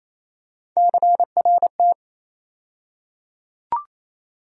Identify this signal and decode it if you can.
Morse code, heard as a keyed tone, about 28 wpm, 700 Hz. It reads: CRT
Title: USA Call Sign Postfixes - 3 Letters 28wpm